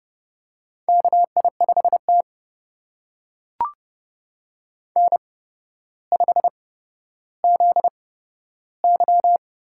Morse code, heard as a keyed tone, about 30 wpm, spaced as for 13 wpm, 700 Hz. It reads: KI5T N5ZY
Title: USA Call Signs - 4 Characters - 8x Character Spacing - 30wpm